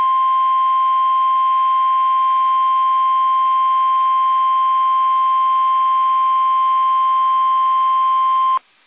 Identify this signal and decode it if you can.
Emergency Alert System (SAME) tones (the 1050 Hz attention tone). It is heard from 0.3 to 8.5 s.
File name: eas / KUSO / KUSO